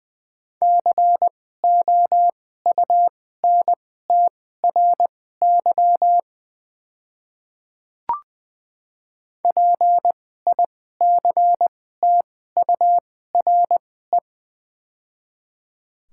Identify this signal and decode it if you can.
Morse code, heard as a keyed tone, about 20 wpm, spaced as for 15 wpm, 700 Hz. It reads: COUNTRY PICTURE
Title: 2x Character Spacing - Top 300 Words - 20wpm